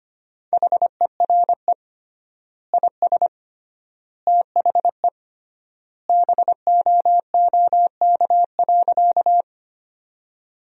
Morse code, heard as a keyed tone, about 25 wpm, 700 Hz. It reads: HERE IS THE BOOK.
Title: 3x Word-Spacing - Sentences from Top 300 Words - 25wpm